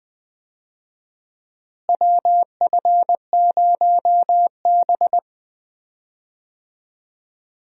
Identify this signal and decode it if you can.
Morse code, heard as a keyed tone, about 21 wpm, 700 Hz. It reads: WF0B
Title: USA Call Signs - 4 Characters - Repeated 3x - 20wpm